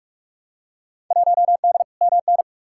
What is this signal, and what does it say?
Morse code, heard as a keyed tone, about 47 wpm, 700 Hz. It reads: 1D MN
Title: Field Day Exchanges - 45wpm